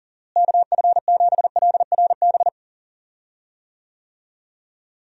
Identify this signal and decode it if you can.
Morse code, heard as a keyed tone, about 40 wpm, 700 Hz. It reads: KF7LRB
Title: USA Call Signs - 6 Characters - Repeated 2x - 40wpm